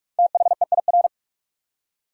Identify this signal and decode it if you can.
Morse code, heard as a keyed tone, about 44 wpm, 700 Hz. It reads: THEIR
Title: Top 180 QSO Elements 45wpm